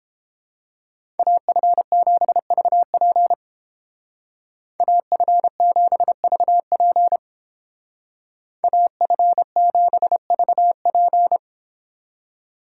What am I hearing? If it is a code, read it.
Morse code, heard as a keyed tone, about 29 wpm, 700 Hz. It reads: AF7VP AF7VP AF7VP